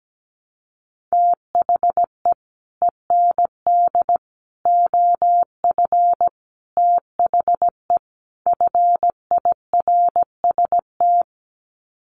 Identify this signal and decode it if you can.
Morse code, heard as a keyed tone, about 17 wpm, 700 Hz. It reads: THE END OF THE FIRST
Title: Sets of 5 Words Encore 17wpm